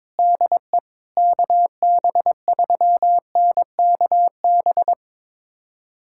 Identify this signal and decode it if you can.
Morse code, heard as a keyed tone, about 22 wpm, 700 Hz. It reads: DE KB3NKB
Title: USA Callsign DE Callsign - 6 Characters - 22wpm